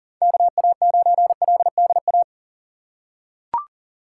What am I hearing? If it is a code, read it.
Morse code, heard as a keyed tone, about 40 wpm, 700 Hz. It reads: KA9LDA
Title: USA Call Signs - 6 Characters - 6x Character Spacing - 40wpm